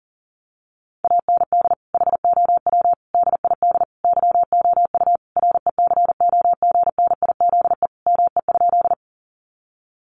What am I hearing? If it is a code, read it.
Morse code, heard as a keyed tone, about 40 wpm, 700 Hz. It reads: AND HOW DID YOU RECOGNIZE ME?
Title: Sentences from Top 1400 Words 40wpm